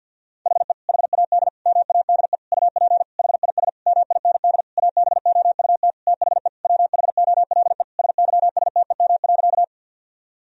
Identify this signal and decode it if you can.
Morse code, heard as a keyed tone, about 50 wpm, 700 Hz. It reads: HE HAD MADE UP HIS MIND ABOUT THE WHOLE SYSTEM.
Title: Sentences from Top 1000 Words - Review Part 3 - 50wpm